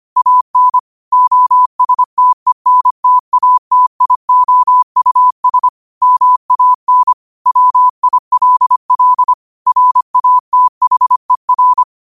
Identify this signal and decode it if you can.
Morse code, heard as a keyed tone, about 25 wpm, 1000 Hz. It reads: AN OSTENTATIOUS MAN WILL RATHER
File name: qotdfiles / 25wpm